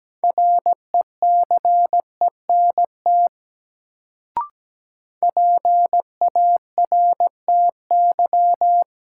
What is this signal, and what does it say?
Morse code, heard as a keyed tone, about 17 wpm, 700 Hz. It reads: RECENT PARTY
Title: Sentences from Top 1400 Words - New Word Drill Alt - 17wpm